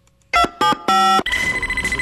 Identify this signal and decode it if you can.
DTMF, heard as telephone touch tones, tones of about 270 ms. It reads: 275